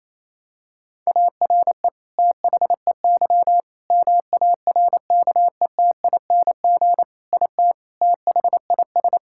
Morse code, heard as a keyed tone, about 28 wpm, 700 Hz. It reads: ARE THEY MARKETING IT THIS
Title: Sentences from Top 1000 Words - Review Part 5 - 28wpm